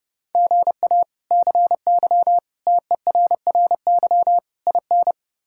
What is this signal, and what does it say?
Morse code, heard as a keyed tone, about 30 wpm, 700 Hz. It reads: GA CY TERRY IN
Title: K1USN SST QSO Practice - 30wpm